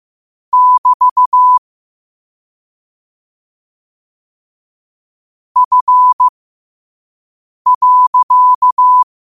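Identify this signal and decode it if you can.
Morse code, heard as a keyed tone, about 15 wpm, spaced as for 5 wpm, 1000 Hz. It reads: <BT> F.
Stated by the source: Morse code sent at 5 wpm.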